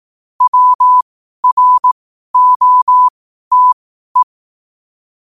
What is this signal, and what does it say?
Morse code, heard as a keyed tone, about 18 wpm, spaced as for 12 wpm, 1000 Hz. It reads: WROTE
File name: qotdfiles / 12wpm